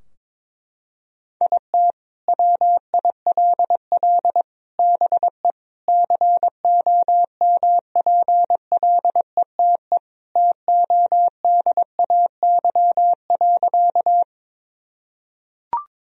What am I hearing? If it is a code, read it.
Morse code, heard as a keyed tone, about 22 wpm, 700 Hz. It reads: IT WILL BE COMPLETE TODAY.
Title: Sentences from Top 1000 Words - Review Part 2 - 22wpm